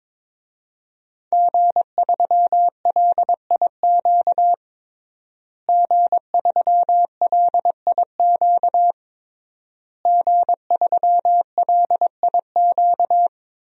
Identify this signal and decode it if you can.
Morse code, heard as a keyed tone, about 21 wpm, 700 Hz. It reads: G3LIQ G3LIQ G3LIQ
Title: DXCC-100 Call Signs - 5 Characters - Repeated 3x - 22wpm